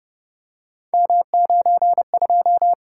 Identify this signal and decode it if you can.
Morse code, heard as a keyed tone, about 32 wpm, 700 Hz. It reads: M92